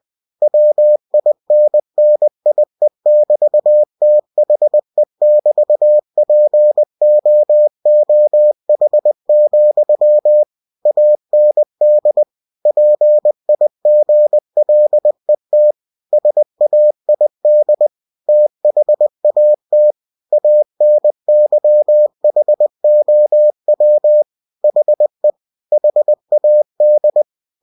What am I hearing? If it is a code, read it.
Morse code, heard as a keyed tone, about 20 wpm, 600 Hz. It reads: WINNIE<BT>THE<BT>POOH, AND PIGLET SAID THAT ANYHOW HE HAD
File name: cw-books / winnie-the-pooh / winnie-20wpm-page-0022